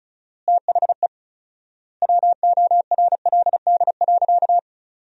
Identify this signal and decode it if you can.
Morse code, heard as a keyed tone, about 34 wpm, 700 Hz. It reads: THE WORLD.
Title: 4x Word-Spacing - Sentences from Top 200 Words - 35wpm